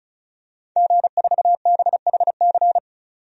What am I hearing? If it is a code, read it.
Morse code, heard as a keyed tone, about 36 wpm, 700 Hz. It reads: G4BHC
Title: DXCC-100 Call Signs - 5 Characters - 35wpm